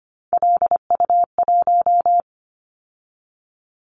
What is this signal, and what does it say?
Morse code, heard as a keyed tone, about 25 wpm, 700 Hz. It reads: LU1
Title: Callsign Prefixes for Top 50 DXCC - 25wpm